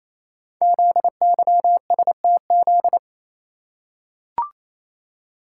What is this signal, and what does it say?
Morse code, heard as a keyed tone, about 27 wpm, 700 Hz. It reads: ZYSTZ
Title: Code Groups - 5 Letters - 28wpm